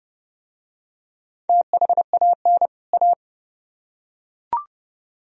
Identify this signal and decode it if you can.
Morse code, heard as a keyed tone, about 25 wpm, 700 Hz. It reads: THAN A